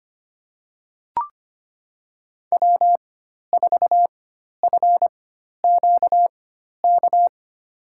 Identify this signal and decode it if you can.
Morse code, heard as a keyed tone, about 25 wpm, spaced as for 12 wpm, 700 Hz. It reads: W4FQK
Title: USA Call Signs - 5 Characters - 4x Character Spacing - 25wpm